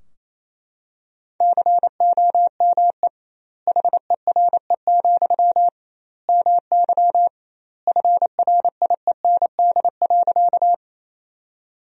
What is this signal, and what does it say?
Morse code, heard as a keyed tone, about 28 wpm, 700 Hz. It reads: COME HERE, MY FRIEND.
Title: 2x Word-Spacing - Sentences from Top 300 Words - 28wpm